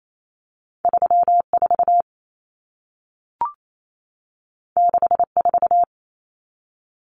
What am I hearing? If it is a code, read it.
Morse code, heard as a keyed tone, about 28 wpm, 700 Hz. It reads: 34 64
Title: Numbers - 2-Digit 28wpm